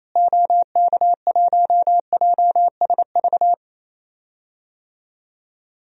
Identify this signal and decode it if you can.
Morse code, heard as a keyed tone, about 28 wpm, 700 Hz. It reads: OK1JSV
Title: DXCC-100 Call Signs - 6 Characters - Repeated 2x - 28wpm